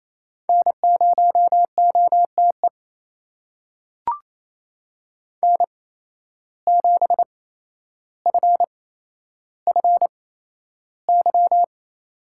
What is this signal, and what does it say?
Morse code, heard as a keyed tone, about 28 wpm, spaced as for 12 wpm, 700 Hz. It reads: N0OTE N7FFY
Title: USA Call Signs - 8x Character Spacing - 28wpm